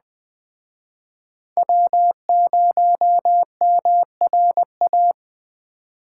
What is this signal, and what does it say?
Morse code, heard as a keyed tone, about 21 wpm, 700 Hz. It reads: W0MRA
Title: Speed Racing - USA Call Signs 17wpm